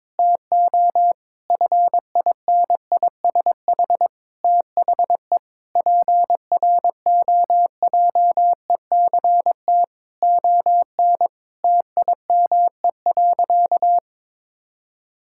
Morse code, heard as a keyed tone, about 22 wpm, 700 Hz. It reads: TO FINISH THE PROJECT ON TIME.
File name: Difficult Sentences with Hints - 22wpm